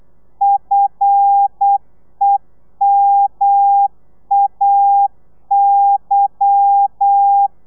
Morse code, heard as a keyed tone, about 8 wpm, 800 Hz. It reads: FEMAY
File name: Lez11